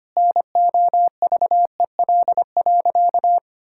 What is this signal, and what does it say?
Morse code, heard as a keyed tone, about 25 wpm, 700 Hz. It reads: NOVEL.
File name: Easy Sentences with Hints - 25wpm